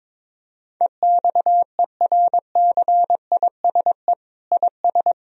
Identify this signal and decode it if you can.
Morse code, heard as a keyed tone, about 22 wpm, 700 Hz. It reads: EXERCISE IS
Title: Sentences from Top 1000 Words - Review Part 4 - 22wpm